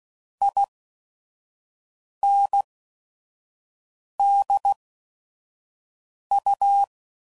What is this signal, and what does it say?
Morse code, heard as a keyed tone, about 16 wpm, 800 Hz. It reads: I N D U